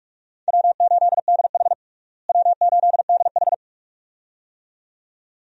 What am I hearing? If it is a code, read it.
Morse code, heard as a keyed tone, about 47 wpm, 700 Hz. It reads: W8DH W8DH
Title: USA Call Signs - 4 Characters - Repeated 2x - 45wpm